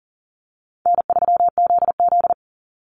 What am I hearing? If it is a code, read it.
Morse code, heard as a keyed tone, about 40 wpm, 700 Hz. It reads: N3ZZ
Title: Call Signs Repeated 3x - 40wpm